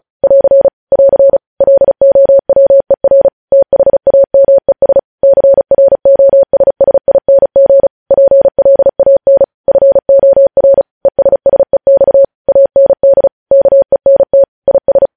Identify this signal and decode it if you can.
Morse code, heard as a keyed tone, about 36 wpm, 550 Hz. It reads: <AR> <AR> LOWER THAMES CROSSING PLAN FOR ESSEX AND KENT IS
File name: podcast / News headlines-1x-35x35